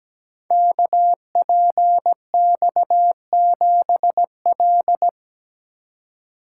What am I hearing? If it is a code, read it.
Morse code, heard as a keyed tone, about 17 wpm, 700 Hz. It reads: KPX7L
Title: Code Groups - 5 Letter:Numbers - 17wpm